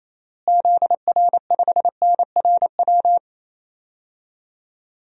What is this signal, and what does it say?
Morse code, heard as a keyed tone, about 28 wpm, 700 Hz. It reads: ZR5NRW
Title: DXCC-100 Call Signs - 6 Characters - Repeated 3x - 28wpm